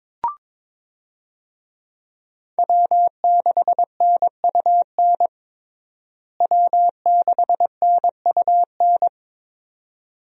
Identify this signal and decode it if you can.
Morse code, heard as a keyed tone, about 22 wpm, 700 Hz. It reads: W6NUN W6NUN
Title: USA Call Signs - 5 Characters - Repeated 2x - 22wpm